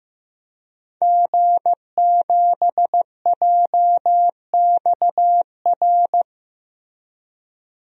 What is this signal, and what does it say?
Morse code, heard as a keyed tone, about 15 wpm, 700 Hz. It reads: G7JXR
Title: DXCC-100 Call Signs - 5 Characters - 15wpm